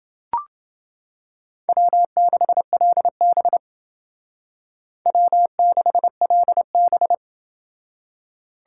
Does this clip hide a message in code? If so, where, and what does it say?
Yes: Morse code from 1.7 to 7.1 s, heard as a keyed tone, about 28 wpm, 700 Hz. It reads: W6LB W6LB